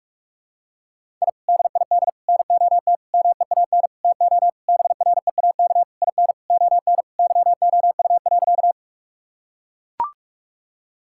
Morse code, heard as a keyed tone, about 41 wpm, 700 Hz. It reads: I DID NOT MEAN TO BREAK IN ON YOU.
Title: Sentences from Top 1000 Words - Review Part 6 - 45wpm